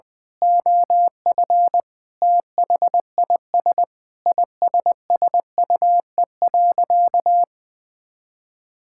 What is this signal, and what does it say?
Morse code, heard as a keyed tone, about 21 wpm, 700 Hz. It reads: OF THIS ISSUE.
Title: Sentences with 7-Word Phrases - 20wpm